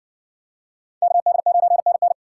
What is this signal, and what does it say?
Morse code, heard as a keyed tone, about 57 wpm, 700 Hz. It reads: KD9NN